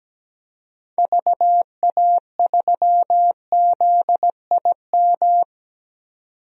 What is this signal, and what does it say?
Morse code, heard as a keyed tone, about 17 wpm, 700 Hz. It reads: VA3ZIM
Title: DXCC-100 Call Signs - 6 Characters - 17wpm